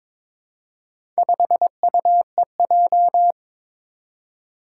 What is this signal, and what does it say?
Morse code, heard as a keyed tone, about 22 wpm, 700 Hz. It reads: 5UEJ
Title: Four Challenging Letter-Number Combinations 22wpm